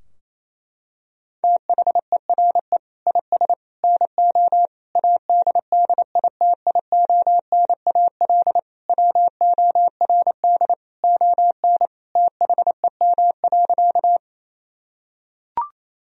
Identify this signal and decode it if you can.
Morse code, heard as a keyed tone, about 28 wpm, 700 Hz. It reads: THERE IS NO ADDITIONAL WORD ON THEM.
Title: Sentences from Top 1000 Words - Review Part 4 - 28wpm